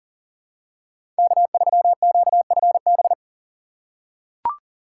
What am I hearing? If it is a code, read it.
Morse code, heard as a keyed tone, about 40 wpm, 700 Hz. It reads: K3QFB
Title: USA Call Signs - 2x Character Spacing - 40wpm